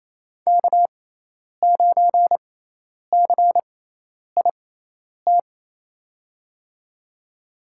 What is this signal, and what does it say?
Morse code, heard as a keyed tone, about 28 wpm, spaced as for 10 wpm, 700 Hz. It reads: K9CIT
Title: USA Call Signs - 6x Character Spacing - 28wpm